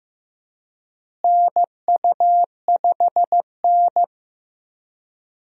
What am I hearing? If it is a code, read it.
Morse code, heard as a keyed tone, about 16 wpm, 700 Hz. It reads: NU5N
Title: USA Call Signs - 4 Characters - Repeated 3x - 15wpm